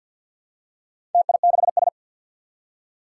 Morse code, heard as a keyed tone, about 47 wpm, 700 Hz. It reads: TI6S
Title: Four Challenging Letter-Number Combinations 50wpm